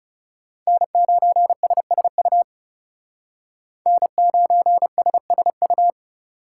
Morse code, heard as a keyed tone, about 36 wpm, 700 Hz. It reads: N9SSU N9SSU
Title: Speed Racing - USA 1x3 Call Signs 30wpm